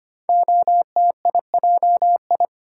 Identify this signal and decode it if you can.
Morse code, heard as a keyed tone, about 25 wpm, 700 Hz. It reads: OTIJI
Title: Code Groups - 5 Letters - 25wpm